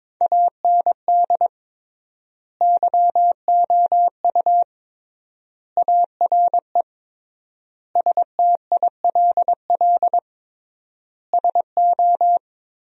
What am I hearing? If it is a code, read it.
Morse code, heard as a keyed tone, about 22 wpm, 700 Hz. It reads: AND YOU ARE STILL SO
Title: 3x Word-Spacing - Sentences from Top 400 Words - 22wpm